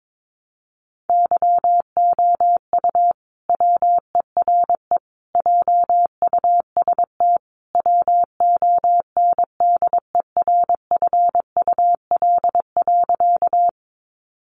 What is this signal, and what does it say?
Morse code, heard as a keyed tone, about 22 wpm, 700 Hz. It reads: YOU WERE JUST WONDERFUL.
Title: Sentences from Top 1100 Words 22wpm